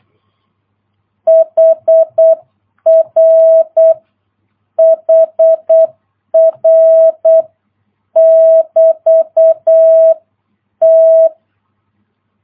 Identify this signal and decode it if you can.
Morse code, heard as a keyed tone, about 8 wpm, 650 Hz. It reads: HR HR <BT> T